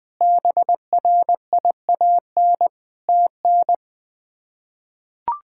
Morse code, heard as a keyed tone, about 20 wpm, 700 Hz. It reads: BRIAN TN
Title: K1USN SST QSO Practice - 20wpm